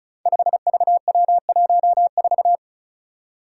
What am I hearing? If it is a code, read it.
Morse code, heard as a keyed tone, about 36 wpm, 700 Hz. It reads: 5VW14
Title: Five Challenging Letter-Number Combinations 35wpm